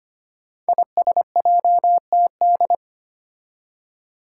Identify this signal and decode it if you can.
Morse code, heard as a keyed tone, about 25 wpm, 700 Hz. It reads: ISJTD
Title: Five Challenging Letter-Number Combinations 25wpm